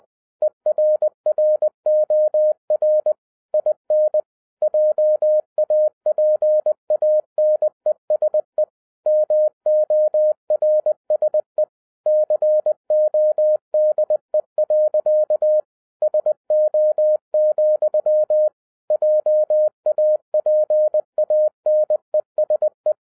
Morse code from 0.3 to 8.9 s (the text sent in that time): ERROR IN JAPANESE